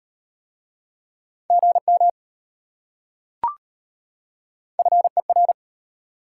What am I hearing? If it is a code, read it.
Morse code, heard as a keyed tone, about 33 wpm, 690 Hz. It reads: GM FER